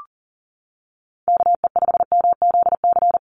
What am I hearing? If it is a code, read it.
Morse code, heard as a keyed tone, about 40 wpm, 690 Hz. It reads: KE5MZC
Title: USA Call Signs 40wpm